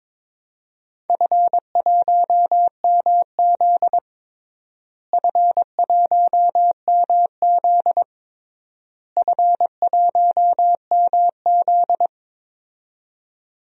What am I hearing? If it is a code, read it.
Morse code, heard as a keyed tone, about 22 wpm, 700 Hz. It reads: F1MZ F1MZ F1MZ
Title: DXCC-100 Call Signs - 4 Characters - Repeated 3x - 22wpm